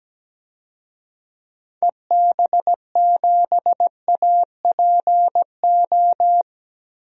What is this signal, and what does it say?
Morse code, heard as a keyed tone, about 17 wpm, 700 Hz. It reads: EB7APO
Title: DXCC-100 Call Signs - 6 Characters - 17wpm